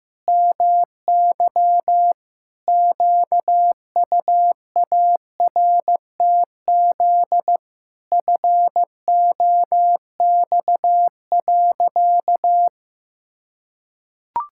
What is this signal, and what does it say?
Morse code, heard as a keyed tone, about 15 wpm, 700 Hz. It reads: MY QUARTZ FOX.
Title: Pangram Challenge v2 - 15wpm